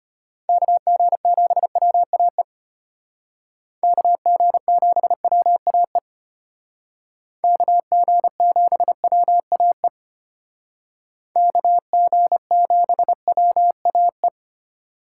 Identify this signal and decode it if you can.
Morse code, heard as a keyed tone, about 33 wpm, 700 Hz. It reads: KG7WAE KG7WAE KG7WAE KG7WAE